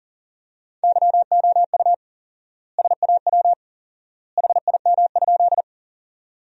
Morse code, heard as a keyed tone, about 40 wpm, 700 Hz. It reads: YOU SAW HIM?
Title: 4x Word-Spacing - Sentences from Top 300 Words - 40wpm